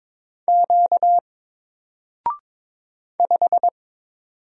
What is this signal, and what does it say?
Morse code, heard as a keyed tone, about 22 wpm, 700 Hz. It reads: Q 5
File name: Single-Letter-Number-22wpm